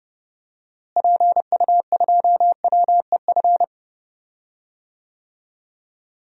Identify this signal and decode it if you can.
Morse code, heard as a keyed tone, about 30 wpm, 700 Hz. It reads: PU2WEF
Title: DXCC-100 Call Signs - 6 Characters - Repeated 3x - 30wpm